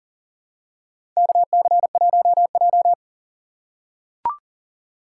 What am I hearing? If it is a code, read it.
Morse code, heard as a keyed tone, about 40 wpm, 700 Hz. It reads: KC1J
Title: USA Call Signs - 4 Characters - 6x Character Spacing - 40wpm